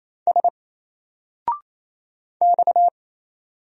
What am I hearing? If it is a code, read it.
Morse code, heard as a keyed tone, about 27 wpm, 700 Hz. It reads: S X
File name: Single-Letter-Number-28wpm